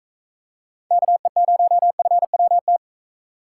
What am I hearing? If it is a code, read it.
Morse code, heard as a keyed tone, about 41 wpm, 700 Hz. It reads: KE0FWT